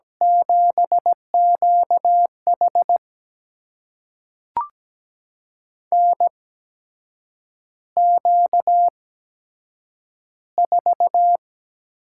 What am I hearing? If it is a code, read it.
Morse code, heard as a keyed tone, about 17 wpm, spaced as for 7 wpm, 700 Hz. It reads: 7QH NQ4
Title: USA Call Signs - 8x Character Spacing - 17wpm